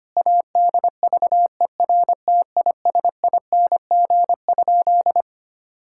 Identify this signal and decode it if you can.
Morse code, heard as a keyed tone, about 25 wpm, 700 Hz. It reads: ADVERTISING?
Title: Sentences from Top 1000 Words - Review Part 6 - 25wpm